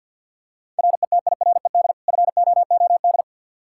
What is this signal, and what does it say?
Morse code, heard as a keyed tone, about 48 wpm, 700 Hz. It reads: RETIRED FOOD
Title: Practice QSO - Exchange 3 - More Info - 50wpm